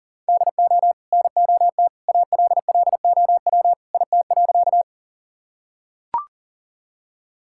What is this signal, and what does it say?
Morse code, heard as a keyed tone, about 37 wpm, 700 Hz. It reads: DO NOT ALLOW IT.
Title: Sentences from Top 1000 Words - Review Part 5 - 40wpm